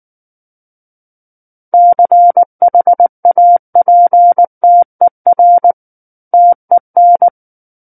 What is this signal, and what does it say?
Morse code, heard as a keyed tone, about 19 wpm, 700 Hz. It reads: CHAPTER TEN
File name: Book One / 10-Chapter 10 - 19 WPM